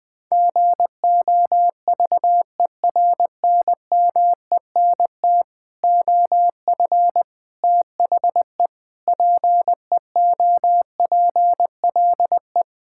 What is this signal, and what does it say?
Morse code, heard as a keyed tone, about 20 wpm, 700 Hz. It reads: GOVERNMENT OF THE PEOPLE
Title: Sentences with 9-Word Phrases - 20wpm